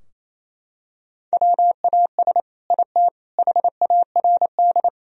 Morse code, heard as a keyed tone, about 28 wpm, 700 Hz. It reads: WAS IT HARD
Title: Sentences from Top 1000 Words - Review Part 4 - 28wpm